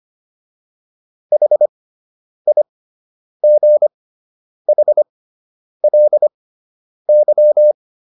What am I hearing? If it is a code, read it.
Morse code, heard as a keyed tone, about 25 wpm, spaced as for 9 wpm, 600 Hz. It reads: HIGHLY